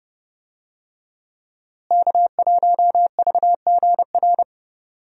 Morse code, heard as a keyed tone, about 31 wpm, 700 Hz. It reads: K1VGR
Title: USA Call Signs - 5 Characters - Repeated 2x - 30wpm